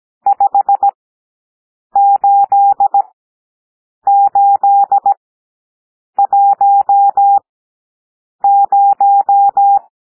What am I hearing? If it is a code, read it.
Morse code, heard as a keyed tone, about 17 wpm, spaced as for 7 wpm, 800 Hz. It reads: 58810
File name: SOUND NUMBERS 17 wpm